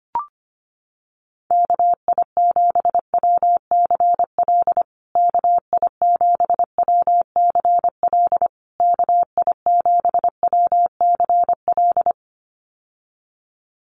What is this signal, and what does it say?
Morse code, heard as a keyed tone, about 25 wpm, 700 Hz. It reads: KI7WCL KI7WCL KI7WCL
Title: USA Call Signs Repeated 3x 25wpm